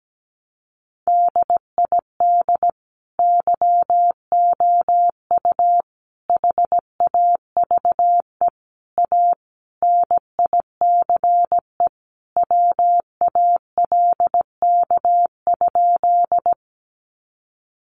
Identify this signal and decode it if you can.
Morse code, heard as a keyed tone, about 17 wpm, 700 Hz. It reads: DID YOU HAVE A NICE WALK?
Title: Sentences from Top 1100 Words 17wpm